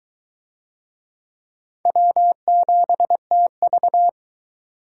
Morse code, heard as a keyed tone, about 19 wpm, 700 Hz. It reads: W7TV